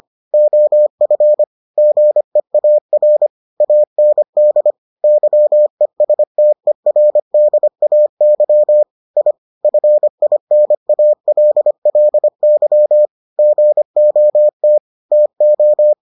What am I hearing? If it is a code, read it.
Morse code, heard as a keyed tone, about 25 wpm, 600 Hz. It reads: OF GEAR AND YESTERDAY I FINALLY GOT TO